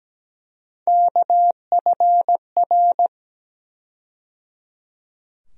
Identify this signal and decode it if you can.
Morse code, heard as a keyed tone, about 17 wpm, 700 Hz. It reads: KFR
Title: USA Call Sign Postfixes - 3 Letters 17wpm